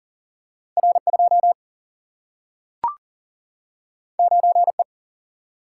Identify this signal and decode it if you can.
Morse code, heard as a keyed tone, about 40 wpm, 700 Hz. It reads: R2 9E
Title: Single-Letter Single-Number 40wpm